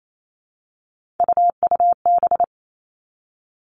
Morse code, heard as a keyed tone, about 28 wpm, 700 Hz. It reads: UUB
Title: Three Challenging Letter-Number Combinations 28wpm